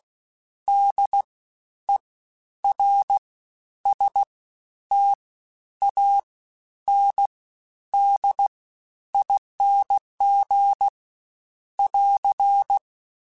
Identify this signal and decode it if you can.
Morse code, heard as a keyed tone, about 16 wpm, 800 Hz. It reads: D E R S T A N D ING <AR>
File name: Special INGs-mp3s / Special INGs-mp3s / SlowToFaster ing @15